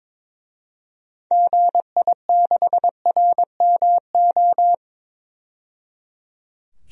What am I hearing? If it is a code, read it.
Morse code, heard as a keyed tone, about 22 wpm, 700 Hz. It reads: GI6RMO
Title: DXCC-100 Call Signs - 6 Characters - 22wpm